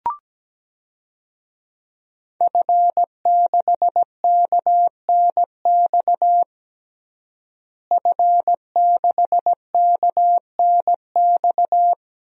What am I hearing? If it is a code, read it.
Morse code, heard as a keyed tone, about 17 wpm, 700 Hz. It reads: F6KNX F6KNX
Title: DXCC-100 Call Signs - 5 Characters - Repeated 3x - 17wpm